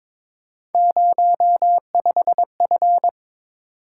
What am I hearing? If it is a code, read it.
Morse code, heard as a keyed tone, about 26 wpm, 700 Hz. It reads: 05F
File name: Speed-Racing - Three Letter-Number Combinations - 22wpm